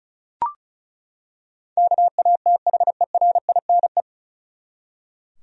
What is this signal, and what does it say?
Morse code, heard as a keyed tone, about 34 wpm, 700 Hz. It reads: KATHERINE
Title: USA Common Names - 35wpm